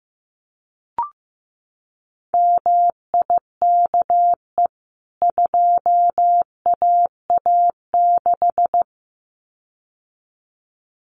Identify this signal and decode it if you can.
Morse code, heard as a keyed tone, about 15 wpm, 700 Hz. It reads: MIKE 2AA6
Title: CWT Exchange 15wpm